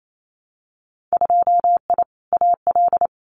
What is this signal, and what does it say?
Morse code, heard as a keyed tone, about 28 wpm, 700 Hz. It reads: 2I AL
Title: Winter Field Day Exchanges 28wpm